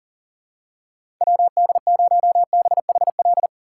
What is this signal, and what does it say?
Morse code, heard as a keyed tone, about 40 wpm, 700 Hz. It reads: WD0BHL
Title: USA Call Signs - 6 Characters - 2x Character Spacing - 40wpm